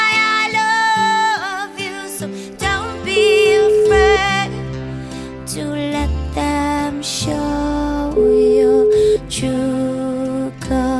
Ringback tone (Europe, 425 Hz), with noise.